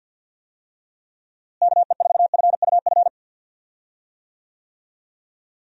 Morse code, heard as a keyed tone, about 52 wpm, 700 Hz. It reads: KE4FUF
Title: USA Call Signs - 6 Characters - Repeated 2x - 50wpm